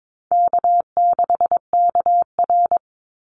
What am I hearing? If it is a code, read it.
Morse code, heard as a keyed tone, about 22 wpm, 700 Hz. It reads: K6KR
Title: Easy Call Signs Repeated 2x - 22wpm